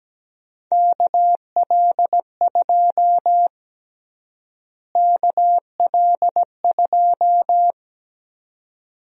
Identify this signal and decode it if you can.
Morse code, heard as a keyed tone, about 17 wpm, 700 Hz. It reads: KL2 KL2
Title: USA Call Sign Prefixes - Repeated 2x - 17wpm